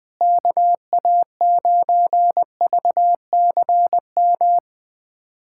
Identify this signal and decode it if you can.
Morse code, heard as a keyed tone, about 21 wpm, 700 Hz. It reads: KA9VCM
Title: USA Callsign DE Callsign 20wpm